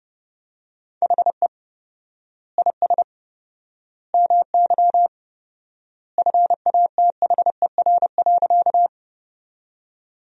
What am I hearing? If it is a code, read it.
Morse code, heard as a keyed tone, about 30 wpm, 700 Hz. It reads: HE IS MY FATHER.
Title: 4x Word-Spacing - Sentences from Top 300 Words - 30wpm